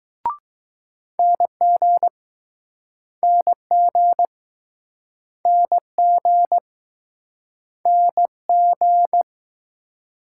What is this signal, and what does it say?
Morse code, heard as a keyed tone, about 21 wpm, 700 Hz. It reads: NG NG NG NG